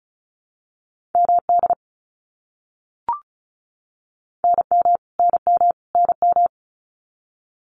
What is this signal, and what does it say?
Morse code, heard as a keyed tone, about 34 wpm, 700 Hz. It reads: MD NM NM NM
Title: US States Repeated 3x 35wpm